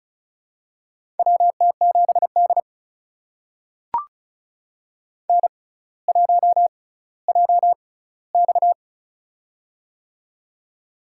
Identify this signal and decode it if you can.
Morse code, heard as a keyed tone, about 36 wpm, spaced as for 18 wpm, 700 Hz. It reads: WT7D N1JX
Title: USA Call Signs - 4 Characters - 6x Character Spacing - 35wpm